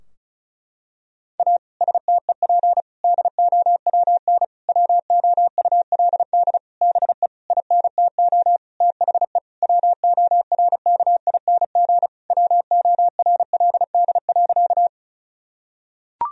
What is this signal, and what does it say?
Morse code, heard as a keyed tone, about 36 wpm, 700 Hz. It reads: A STEP DOWN WOULD BE INTO THE WORKING WORLD.
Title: Sentences from Top 1000 Words - Review Part 4 - 35wpm